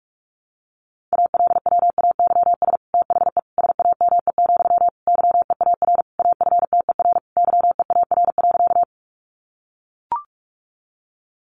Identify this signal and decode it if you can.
Morse code, heard as a keyed tone, about 44 wpm, 700 Hz. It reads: ALWAYS THE SAME, YEAR AFTER YEAR.
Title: Sentences from Top 300 Words 45wpm